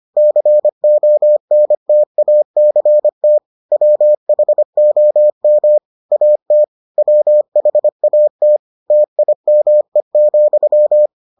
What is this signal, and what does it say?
Morse code, heard as a keyed tone, about 25 wpm, 600 Hz. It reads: CONTACT WHOM AT WHAT TIME,